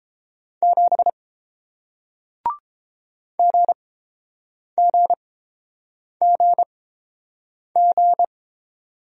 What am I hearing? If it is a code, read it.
Morse code, heard as a keyed tone, about 30 wpm, spaced as for 7 wpm, 700 Hz. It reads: 7 GGGG